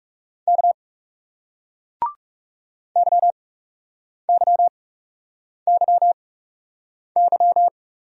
Morse code, heard as a keyed tone, about 39 wpm, spaced as for 8 wpm, 700 Hz. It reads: K YYYY